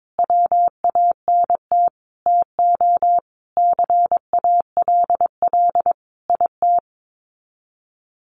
Morse code, heard as a keyed tone, about 22 wpm, 700 Hz. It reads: WANT TO CALL IT
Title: Sets of 5 Words Encore2 22wpm